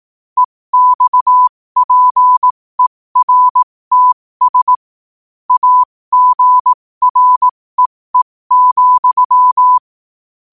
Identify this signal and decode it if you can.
Morse code, heard as a keyed tone, about 18 wpm, spaced as for 15 wpm, 1000 Hz. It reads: EXPERTS AGREE,
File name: qotdfiles / 15wpm